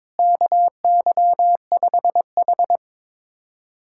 Morse code, heard as a keyed tone, about 22 wpm, 700 Hz. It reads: KY5H
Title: USA Callsign DE Callsign - 4 Characters - 22wpm